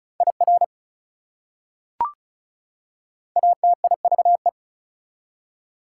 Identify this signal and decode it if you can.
Morse code, heard as a keyed tone, about 36 wpm, 700 Hz. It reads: IR ATIVE
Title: Prefixes and Suffixes 35wpm